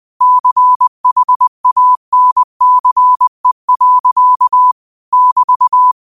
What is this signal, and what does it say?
Morse code, heard as a keyed tone, about 20 wpm, 1000 Hz. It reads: CHANCE. <BT>
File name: qotdfiles / 20wpm